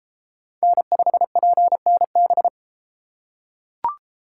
Morse code, heard as a keyed tone, about 32 wpm, 700 Hz. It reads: N5PNB